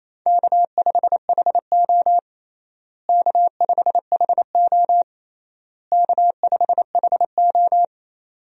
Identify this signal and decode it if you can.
Morse code, heard as a keyed tone, about 28 wpm, 700 Hz. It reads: K5HO K5HO K5HO
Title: USA Call Signs - 4 Characters - Repeated 3x - 28wpm